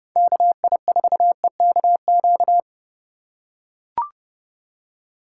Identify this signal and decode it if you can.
Morse code, heard as a keyed tone, about 30 wpm, 700 Hz. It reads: KI4EKQ
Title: USA Call Signs - 6 Characters - 2x Character Spacing - 30wpm